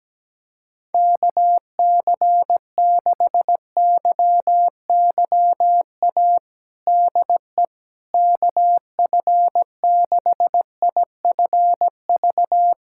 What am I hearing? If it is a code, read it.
Morse code, heard as a keyed tone, about 17 wpm, 700 Hz. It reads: KC6YYA DE KF6IFV
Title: USA Callsign DE Callsign - 6 Characters - 17wpm